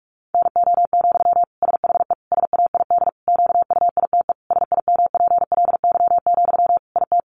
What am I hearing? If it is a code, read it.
Morse code, heard as a keyed tone, about 44 wpm, 700 Hz. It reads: NO, SHE SAID QUITE SIMPLY, IT
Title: Sentences from Top 800 Words 45wpm